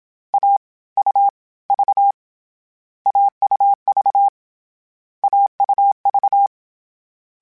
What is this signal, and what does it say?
Morse code, heard as a keyed tone, about 27 wpm, spaced as for 21 wpm, 800 Hz. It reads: AUV AUV AUV
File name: Difficult Combos / Difficult Combos-mp3s / AUV ShortRun @25